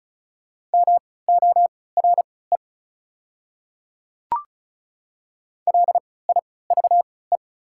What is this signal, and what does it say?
Morse code, heard as a keyed tone, about 36 wpm, spaced as for 20 wpm, 690 Hz. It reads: MORE LIVE
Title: 3x Character Spacing - Top 200 Words - 35wpm